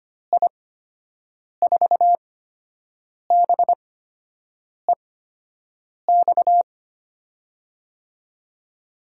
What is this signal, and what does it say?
Morse code, heard as a keyed tone, about 25 wpm, spaced as for 7 wpm, 700 Hz. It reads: I4BEX
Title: USA Call Signs - 6 Characters - 8x Character Spacing - 25wpm